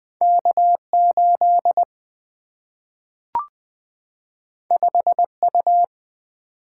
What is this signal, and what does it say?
Morse code, heard as a keyed tone, about 20 wpm, 700 Hz. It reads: K 8 5 U
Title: Two Letter-Number Combinations - 20wpm